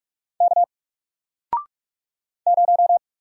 Morse code, heard as a keyed tone, about 44 wpm, 700 Hz. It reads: K 0
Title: Single Letter-Number Alt 45wpm